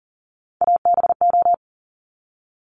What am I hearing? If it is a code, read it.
Morse code, heard as a keyed tone, about 41 wpm, 700 Hz. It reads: ABO
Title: Top Three Letter Combinations in Words 40wpm